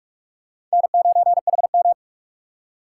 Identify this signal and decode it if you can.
Morse code, heard as a keyed tone, about 46 wpm, 700 Hz. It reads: N9HM
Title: USA Call Signs - 4 Characters - 6x Character Spacing - 45wpm